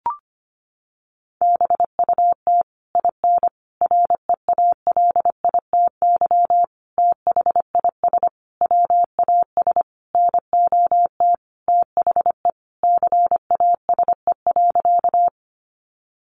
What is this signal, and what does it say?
Morse code, heard as a keyed tone, about 25 wpm, 700 Hz. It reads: BUT IN REALITY THIS WAS NOT THE CASE.
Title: Sentences from Top 1200 Words 25wpm